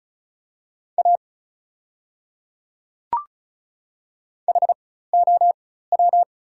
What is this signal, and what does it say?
Morse code, heard as a keyed tone, about 36 wpm, spaced as for 16 wpm, 700 Hz. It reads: A HOW
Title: 4x Character Spacing - Top 200 Words - 35wpm